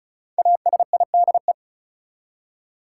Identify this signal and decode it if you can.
Morse code, heard as a keyed tone, about 34 wpm, 700 Hz. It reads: ASIDE
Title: Five Letter Words - 35wpm